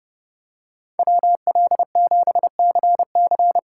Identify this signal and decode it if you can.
Morse code, heard as a keyed tone, about 30 wpm, 700 Hz. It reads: WL7CC
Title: USA Call Signs - 5 Characters - 3x Character Spacing - 30wpm